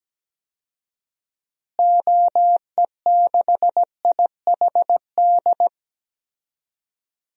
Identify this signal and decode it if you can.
Morse code, heard as a keyed tone, about 17 wpm, 700 Hz. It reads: OE6IHD
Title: DXCC-100 Call Signs - 6 Characters - Repeated 2x - 17wpm